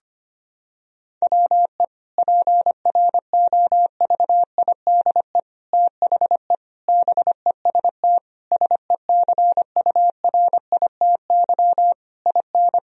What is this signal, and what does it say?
Morse code, heard as a keyed tone, about 25 wpm, 700 Hz. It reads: WE PROVIDE THE BEST SECURITY IN
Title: Sentences from Top 1000 Words - Review Part 4 - 25wpm